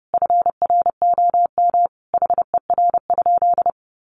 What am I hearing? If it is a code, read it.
Morse code, heard as a keyed tone, about 30 wpm, 700 Hz. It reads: FROM HER?
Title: Sentences from Top 1400 Words 30wpm